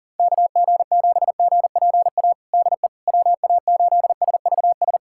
Morse code, heard as a keyed tone, about 40 wpm, 700 Hz. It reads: KG7GPA DE WA8SVS
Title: USA Callsign DE Callsign 40wpm